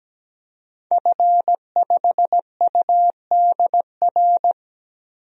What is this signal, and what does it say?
Morse code, heard as a keyed tone, about 17 wpm, 700 Hz. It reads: F5UDR
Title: DXCC-100 Call Signs - 5 Characters - 17wpm